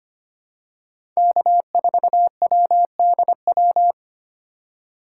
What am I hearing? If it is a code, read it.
Morse code, heard as a keyed tone, about 25 wpm, 700 Hz. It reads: K4WDW
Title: USA Call Signs - 5 Characters - 2x Character Spacing - 25wpm